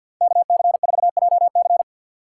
Morse code, heard as a keyed tone, about 53 wpm, 700 Hz. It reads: KC4JC
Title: USA Call Signs - 5 Characters - 3x Character Spacing - 50wpm